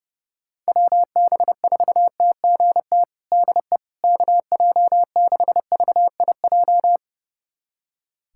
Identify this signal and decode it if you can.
Morse code, heard as a keyed tone, about 30 wpm, 700 Hz. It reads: WB4TGT DE KJ6VIJ
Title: USA Callsign DE Callsign - 6 Characters - 30wpm